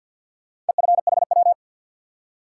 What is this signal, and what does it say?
Morse code, heard as a keyed tone, about 53 wpm, 700 Hz. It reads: EFHW
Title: Top 180 QSO Elements 50wpm